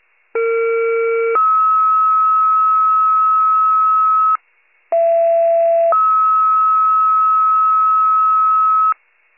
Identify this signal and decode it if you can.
Two-tone page, heard as a one-second tone, then a three-second tone, about 460 then 1290 Hz, about 670 then 1290 Hz. It is heard from 0.3 to 8.9 s.
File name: pager / 78 FIRE / wcalert